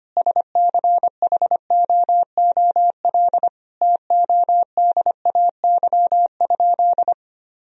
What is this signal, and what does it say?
Morse code, heard as a keyed tone, about 25 wpm, 700 Hz. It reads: SCHOOL TODAY?
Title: Sentences from Top 1000 Words - Review Part 4 - 25wpm